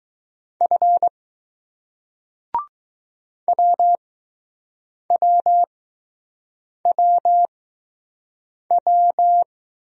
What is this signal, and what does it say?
Morse code, heard as a keyed tone, about 20 wpm, spaced as for 6 wpm, 700 Hz. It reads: F WWWW